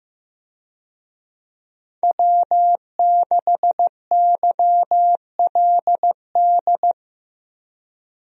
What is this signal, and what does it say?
Morse code, heard as a keyed tone, about 16 wpm, 700 Hz. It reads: W6YLD
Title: USA Call Signs - 5 Characters - Repeated 3x - 15wpm